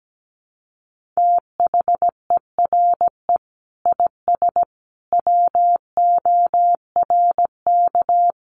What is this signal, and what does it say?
Morse code, heard as a keyed tone, about 17 wpm, 700 Hz. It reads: THERE IS WORK
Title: Sentences from Top 1400 Words 17wpm